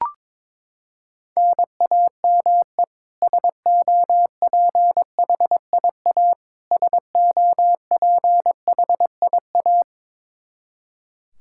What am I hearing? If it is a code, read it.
Morse code, heard as a keyed tone, about 22 wpm, 700 Hz. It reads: NAME SOPHIA SOPHIA
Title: Practice QSO - Exchange 1 - Beginning - 22wpm